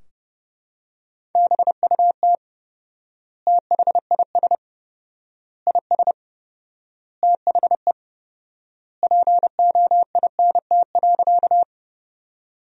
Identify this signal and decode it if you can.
Morse code, heard as a keyed tone, about 30 wpm, 700 Hz. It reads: BUT THIS IS THE POINT.
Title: 4x Word-Spacing - Sentences from Top 200 Words - 30wpm